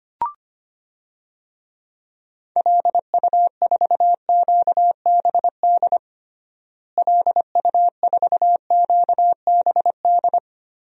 Morse code, heard as a keyed tone, about 25 wpm, 700 Hz. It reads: LU4QBD LU4QBD
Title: DXCC-100 Call Signs - 6 Characters - Repeated 3x - 25wpm